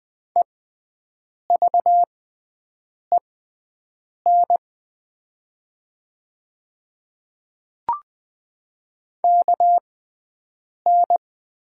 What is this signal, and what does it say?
Morse code, heard as a keyed tone, about 20 wpm, spaced as for 7 wpm, 700 Hz. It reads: EVEN KN
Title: 6x Character Spacing - Top 200 Words - 20wpm